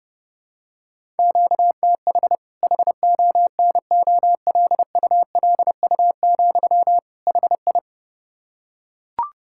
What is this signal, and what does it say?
Morse code, heard as a keyed tone, about 30 wpm, 700 Hz. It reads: QTH HONOLULU, HI
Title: K1USN SST QSO Practice 30wpm